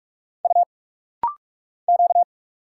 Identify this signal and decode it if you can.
Morse code, heard as a keyed tone, about 46 wpm, 700 Hz. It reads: U Q
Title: Single Letters - Rapid-Fire 45wpm